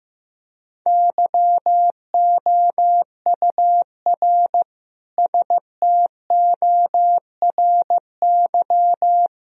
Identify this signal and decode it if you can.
Morse code, heard as a keyed tone, about 15 wpm, 700 Hz. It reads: YOUR STORY
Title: Sentences from Top 1000 Words - Review Part 2 - 15wpm